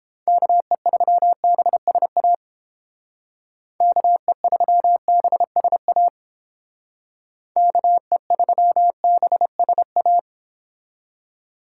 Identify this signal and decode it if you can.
Morse code, heard as a keyed tone, about 28 wpm, 700 Hz. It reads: KE3BSA KE3BSA KE3BSA